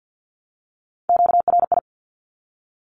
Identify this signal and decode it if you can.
Morse code, heard as a keyed tone, about 50 wpm, 700 Hz. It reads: QRI
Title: Q-Codes Repeated 3x 50wpm